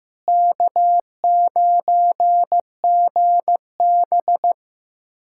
Morse code, heard as a keyed tone, about 15 wpm, 700 Hz. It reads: K9GB
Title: USA Callsign DE Callsign - 4 Characters - 15wpm